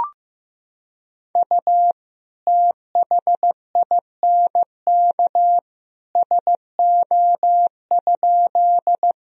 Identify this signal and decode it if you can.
Morse code, heard as a keyed tone, about 15 wpm, 700 Hz. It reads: U THINK SO?
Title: Sentences from Top 200 Words and Abbreviations 15wpm